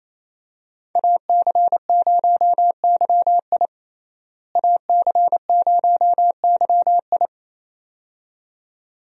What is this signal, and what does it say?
Morse code, heard as a keyed tone, about 28 wpm, 700 Hz. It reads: AC0YI AC0YI
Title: USA Call Signs - 5 Characters - Repeated 2x - 28wpm